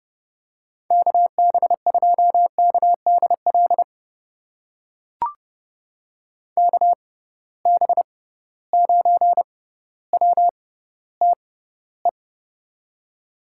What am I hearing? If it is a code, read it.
Morse code, heard as a keyed tone, about 30 wpm, spaced as for 15 wpm, 700 Hz. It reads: KB2KDL KB9WTE
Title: USA Call Signs - 6 Characters - 6x Character Spacing - 30wpm